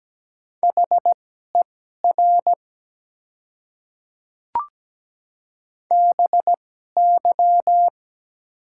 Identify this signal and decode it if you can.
Morse code, heard as a keyed tone, about 17 wpm, spaced as for 12 wpm, 700 Hz. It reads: HER BY
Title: 2x Character Spacing - Top 200 Words - 17wpm